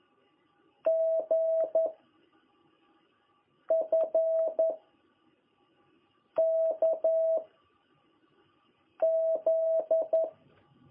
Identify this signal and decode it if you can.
Morse code, heard as a keyed tone, about 11 wpm, 660 Hz. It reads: G F K Z
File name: jh8cbh / 2021cw